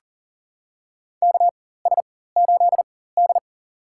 Morse code, heard as a keyed tone, about 40 wpm, spaced as for 19 wpm, 700 Hz. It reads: KS8D
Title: USA Call Signs - 4x Character Spacing - 40wpm